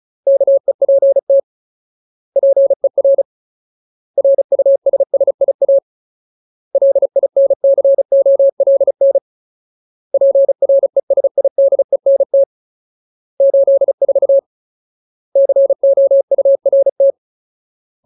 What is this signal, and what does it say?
Morse code, heard as a keyed tone, about 37 wpm, 550 Hz. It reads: KEPT PER RUSSIA LINCOLN PRESIDENT 84 COURT
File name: Ultimate Word List / 35WPM Ultimate